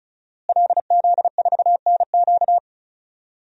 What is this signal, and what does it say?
Morse code, heard as a keyed tone, about 36 wpm, 690 Hz. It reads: LZ4NQ
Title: DXCC-100 Call Signs - 5 Characters - Repeated 3x - 35wpm